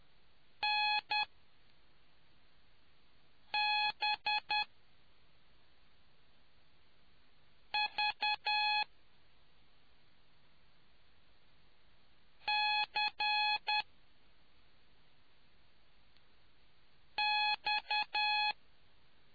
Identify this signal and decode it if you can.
Morse code, heard as a keyed tone, about 10 wpm, spaced as for 3 wpm, 840 Hz. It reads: NBVCX